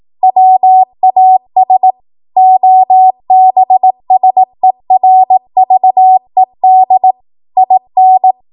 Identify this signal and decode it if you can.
Morse code, heard as a keyed tone, about 18 wpm, 750 Hz. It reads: WAS OBSERVED IN
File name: Morse / 18WPM